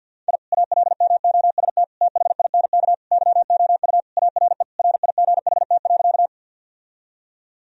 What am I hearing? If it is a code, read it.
Morse code, heard as a keyed tone, about 50 wpm, 700 Hz. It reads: I ALMOST THINK YOU ARE RIGHT.
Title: Sentences from Top 1000 Words - Review Part 4 - 50wpm